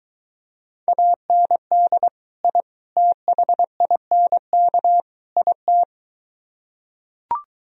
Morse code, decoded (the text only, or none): AND I THINK IT